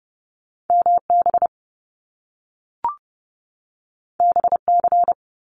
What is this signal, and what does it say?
Morse code, heard as a keyed tone, about 30 wpm, 700 Hz. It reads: MB BC
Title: Canadian Provinces and Territories 30wpm